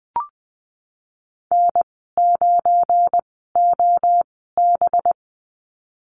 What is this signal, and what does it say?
Morse code, heard as a keyed tone, about 20 wpm, spaced as for 15 wpm, 700 Hz. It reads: N9OB
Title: 193 Lesson 31g - Letter v - Call Signs 20wpm